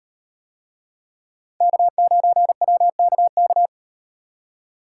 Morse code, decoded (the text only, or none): K9WKK